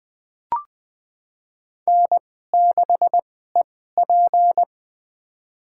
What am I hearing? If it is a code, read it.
Morse code, heard as a keyed tone, about 20 wpm, spaced as for 15 wpm, 700 Hz. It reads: N6EP
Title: USA Call Signs - 4 Characters - 2x Character Spacing - 20wpm